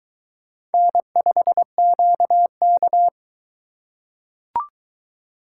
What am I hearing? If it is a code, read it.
Morse code, heard as a keyed tone, about 22 wpm, 700 Hz. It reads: N5QK